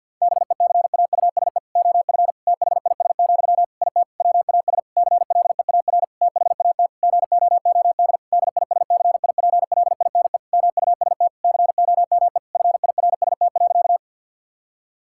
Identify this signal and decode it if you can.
Morse code, heard as a keyed tone, about 50 wpm, 700 Hz. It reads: BECAUSE OF THIS, IT WAS CLEAR THAT GOOD DISCIPLINE MUST COME FIRST.
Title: Sentences from Top 1000 Words - Review Part 6 - 50wpm